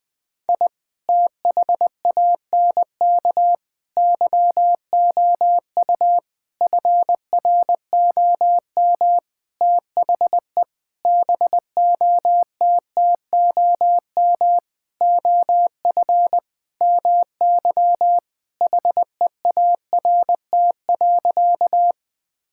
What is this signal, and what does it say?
Morse code, heard as a keyed tone, about 21 wpm, 700 Hz. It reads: I THANK YOU FROM THE BOTTOM OF MY HEART.
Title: Sentences with 6-Word Phrases - 20wpm